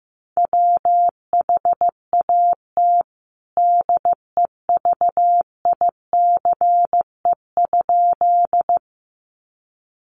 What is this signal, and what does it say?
Morse code, heard as a keyed tone, about 15 wpm, 700 Hz. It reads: WHAT DEVICE?
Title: Sentences from Top 1500 Words 15wpm